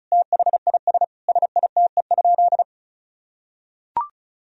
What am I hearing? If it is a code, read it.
Morse code, heard as a keyed tone, about 34 wpm, 700 Hz. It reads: THIS SITE?
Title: Sentences from Top 1000 Words - Review Part 4 - 35wpm